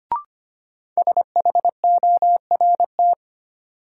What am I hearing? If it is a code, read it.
Morse code, heard as a keyed tone, about 25 wpm, 700 Hz. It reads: SHORT
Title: Sets of 5 Words Encore - New Word Drill - 25wpm